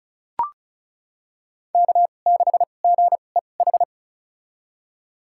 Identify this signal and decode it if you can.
Morse code, heard as a keyed tone, about 34 wpm, spaced as for 25 wpm, 700 Hz. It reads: K6GEH
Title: USA Call Signs - 5 Characters - 2x Character Spacing - 35wpm